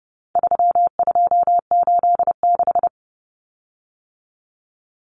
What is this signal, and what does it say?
Morse code, heard as a keyed tone, about 30 wpm, 700 Hz. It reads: 3286
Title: Common Cut Numbers - 4-Digit 30wpm